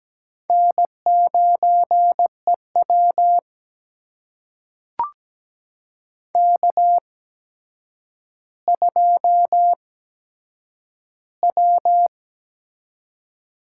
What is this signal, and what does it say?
Morse code, heard as a keyed tone, about 17 wpm, spaced as for 8 wpm, 700 Hz. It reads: N9EW K2W
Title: USA Call Signs - 4 Characters - 8x Character Spacing - 17wpm